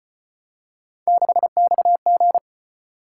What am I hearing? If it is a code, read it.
Morse code, heard as a keyed tone, about 34 wpm, 700 Hz. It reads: 6XG